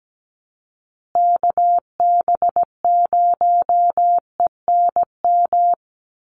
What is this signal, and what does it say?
Morse code, heard as a keyed tone, about 17 wpm, 700 Hz. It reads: KB0ENM
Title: USA Call Signs Repeated 2x 17wpm